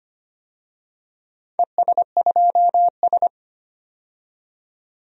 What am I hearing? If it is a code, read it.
Morse code, heard as a keyed tone, about 25 wpm, 700 Hz. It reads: ES2S
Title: DXCC-100 Call Signs - 4 Characters - 25wpm